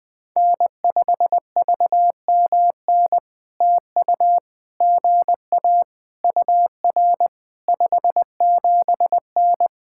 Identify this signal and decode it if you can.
Morse code, heard as a keyed tone, about 21 wpm, 700 Hz. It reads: N5VMN TU GA UR 57N
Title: POTA QSO Practice - 20wpm